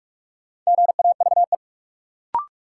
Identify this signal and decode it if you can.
Morse code, heard as a keyed tone, about 47 wpm, 700 Hz. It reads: GAVE
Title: Sentences from Top 500 Words - New Word Drill - 45wpm